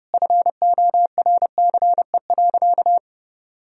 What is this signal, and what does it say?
Morse code, heard as a keyed tone, about 30 wpm, 700 Hz. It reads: FORCE.
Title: Sentences from Top 1000 Words - Review Part 4 - 30wpm